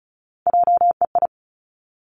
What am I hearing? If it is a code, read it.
Morse code, heard as a keyed tone, about 36 wpm, 700 Hz. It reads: JEI
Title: Three Challenging Letter-Number Combinations 35wpm